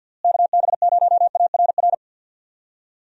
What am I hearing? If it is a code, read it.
Morse code, heard as a keyed tone, about 50 wpm, 700 Hz. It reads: KB0ARR
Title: USA Call Signs - 6 Characters - Repeated 3x - 50wpm